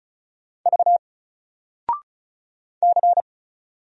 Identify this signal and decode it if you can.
Morse code, heard as a keyed tone, about 36 wpm, 700 Hz. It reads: V C
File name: Single-Letter-Number-35wpm